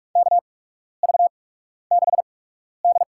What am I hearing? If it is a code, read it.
Morse code, heard as a keyed tone, about 44 wpm, spaced as for 12 wpm, 700 Hz. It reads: KV6D
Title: USA Call Signs - 4 Characters - 8x Character Spacing - 45wpm